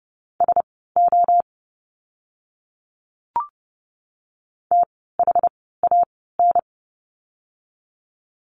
Morse code, heard as a keyed tone, about 30 wpm, spaced as for 17 wpm, 700 Hz. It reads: SO THAN
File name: course / 30:10wpm / 237 Lesson 38f - Letter q - Words 30wpm